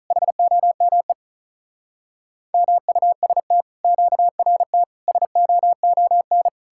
Morse code, heard as a keyed tone, about 47 wpm, 700 Hz. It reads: HOME MUST QRT SOON